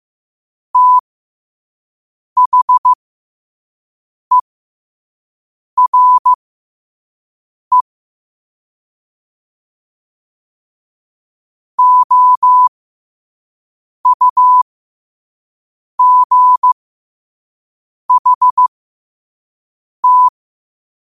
Morse code, heard as a keyed tone, about 15 wpm, spaced as for 5 wpm, 1000 Hz. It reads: THERE OUGHT
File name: qotdfiles / 5wpm